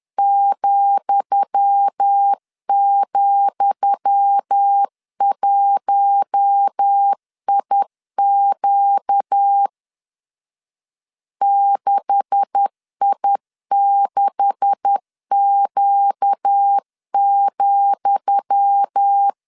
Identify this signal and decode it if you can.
Morse code, heard as a keyed tone, about 10 wpm, 800 Hz. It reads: ,,1IQ 6I6Q,
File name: LEZIONI / 10 SIGNS / 10D COMA 10wpm 3013